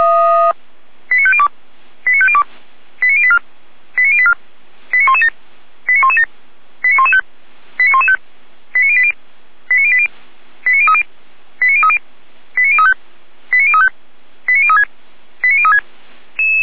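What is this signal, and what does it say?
Five-tone selective call (ZVEI), heard as quick runs of five tones, tones of about 70 ms. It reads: 89502 89502 89084 89084 89117 89117 89115 89115 89080 89080 89030 89030 89035 89035 89036 89036
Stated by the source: ZVEI five-tone selective call 89094 received with an SDR.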